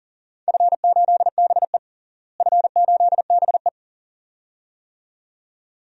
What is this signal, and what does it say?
Morse code, heard as a keyed tone, about 40 wpm, 700 Hz. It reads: F8BE F8BE
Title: DXCC-100 Call Signs - 4 Characters - Repeated 2x - 40wpm